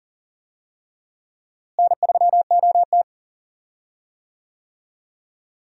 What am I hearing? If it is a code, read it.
Morse code, heard as a keyed tone, about 40 wpm, 700 Hz. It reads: N3OT
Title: USA Call Signs - 4 Characters - Repeated 3x - 40wpm